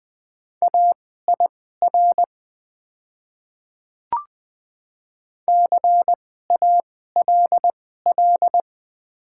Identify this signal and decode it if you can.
Morse code, heard as a keyed tone, about 20 wpm, spaced as for 15 wpm, 700 Hz. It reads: AIR CALL
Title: 2x Character Spacing - Top 300 Words - 20wpm